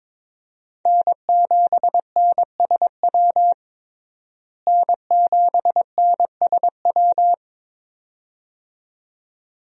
Morse code, heard as a keyed tone, about 22 wpm, 700 Hz. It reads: N7NSW N7NSW
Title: USA Call Signs - 5 Characters - Repeated 2x - 22wpm